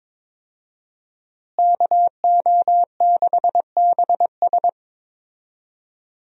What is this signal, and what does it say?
Morse code, heard as a keyed tone, about 22 wpm, 700 Hz. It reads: KO6BS
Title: USA Call Signs - 5 Characters - Repeated 3x - 22wpm